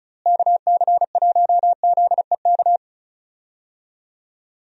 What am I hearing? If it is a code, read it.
Morse code, heard as a keyed tone, about 36 wpm, 700 Hz. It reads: KC1ZEK
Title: USA Call Signs - 6 Characters - Repeated 3x - 35wpm